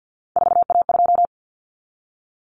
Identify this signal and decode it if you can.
Morse code, heard as a keyed tone, about 50 wpm, 700 Hz. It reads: 4A2
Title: Common Cut Numbers - 3-Digit 50wpm